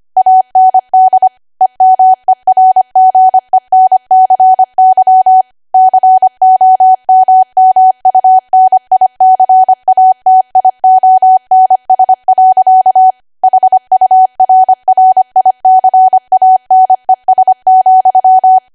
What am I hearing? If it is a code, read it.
Morse code, heard as a keyed tone, about 25 wpm, 750 Hz. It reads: AND EMERGENCY COMMUNICATIONS. HURRICANES,
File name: Archive / 25 WPM / 25WPM